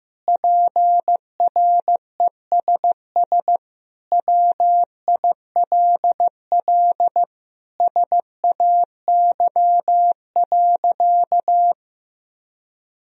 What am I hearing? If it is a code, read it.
Morse code, heard as a keyed tone, about 15 wpm, 700 Hz. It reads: PRESS WILL SAY.
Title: Sentences from Top 1000 Words - Review Part 1 - 15wpm